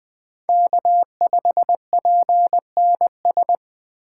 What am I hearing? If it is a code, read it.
Morse code, heard as a keyed tone, about 20 wpm, 700 Hz. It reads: K5PNS
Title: Speed Racing - USA 1x3 Call Signs 17wpm